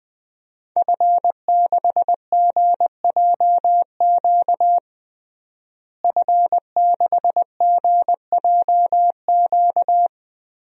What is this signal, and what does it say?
Morse code, heard as a keyed tone, about 21 wpm, 700 Hz. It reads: F6GJQ F6GJQ
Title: DXCC-100 Call Signs - 5 Characters - Repeated 2x - 20wpm